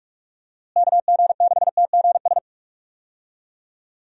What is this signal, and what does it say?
Morse code, heard as a keyed tone, about 46 wpm, 700 Hz. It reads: KG6TGS
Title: USA Call Signs - 6 Characters - Repeated 2x - 45wpm